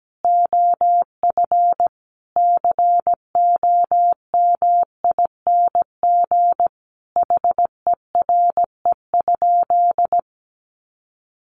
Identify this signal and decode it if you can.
Morse code, heard as a keyed tone, about 17 wpm, 700 Hz. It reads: OF COMING HERE?
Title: Sentences from Top 1300 Words 17wpm